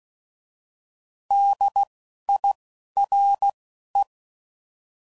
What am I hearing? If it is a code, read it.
Morse code, heard as a keyed tone, about 16 wpm, spaced as for 11 wpm, 800 Hz. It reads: DIRE
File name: Prefixes / By First Letter / Ds